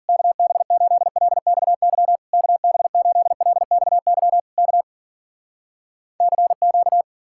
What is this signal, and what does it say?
Morse code, heard as a keyed tone, about 59 wpm, 700 Hz. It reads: KB8LXY KB8LXY K CQ